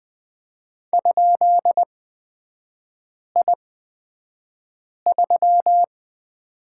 Morse code, heard as a keyed tone, about 20 wpm, spaced as for 5 wpm, 700 Hz. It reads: ?I3
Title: 224 Lesson 36e - BT - Three Characters 20wpm